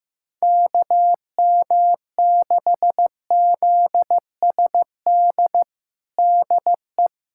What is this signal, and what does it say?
Morse code, heard as a keyed tone, about 15 wpm, 700 Hz. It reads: KM6ZSD DE
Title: USA Callsign DE Callsign - 6 Characters - 15wpm